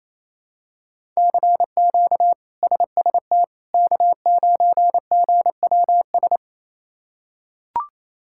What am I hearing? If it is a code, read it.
Morse code, heard as a keyed tone, about 28 wpm, 700 Hz. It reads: CQ SST K9GWS
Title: K1USN SST QSO Practice - 28wpm